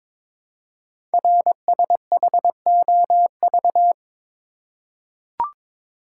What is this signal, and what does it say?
Morse code, heard as a keyed tone, about 22 wpm, 700 Hz. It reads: RSHOV
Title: Code Groups - 5 Letter:Numbers - 22wpm